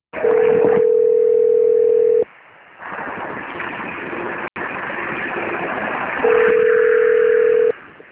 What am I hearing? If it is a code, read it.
Ringback tone (North America), 440 + 480 Hz, with noise.